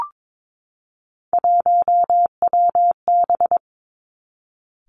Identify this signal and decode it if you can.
Morse code, heard as a keyed tone, about 22 wpm, 700 Hz. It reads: 1WB
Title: Three Challenging Letter-Number Combinations 22wpm